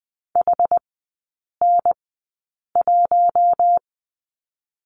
Morse code, heard as a keyed tone, about 19 wpm, spaced as for 8 wpm, 700 Hz. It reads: HN1
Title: 166 Lesson 27e - Number 7 - Three Characters 20wpm